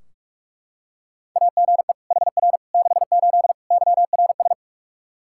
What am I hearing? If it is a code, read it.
Morse code, heard as a keyed tone, about 42 wpm, 700 Hz. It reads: AGE HR 68 YRS
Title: Practice QSO - Exchange 3 - More Info - 45wpm